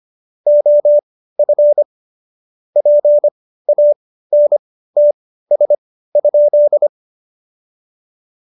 Morse code, heard as a keyed tone, about 25 wpm, spaced as for 15 wpm, 590 Hz. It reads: OF PANTS?
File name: Jokes-25-char-15-eff